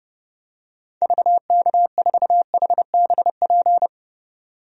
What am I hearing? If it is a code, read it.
Morse code, heard as a keyed tone, about 32 wpm, 700 Hz. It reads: VK4HBP
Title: DXCC-100 Call Signs - 6 Characters - 30wpm